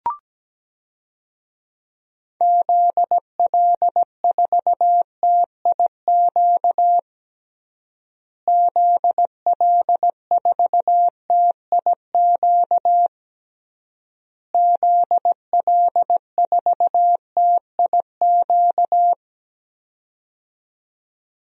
Morse code, heard as a keyed tone, about 17 wpm, 700 Hz. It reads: ZL4TIQ ZL4TIQ ZL4TIQ
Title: DXCC-100 Call Signs - 6 Characters - Repeated 3x - 17wpm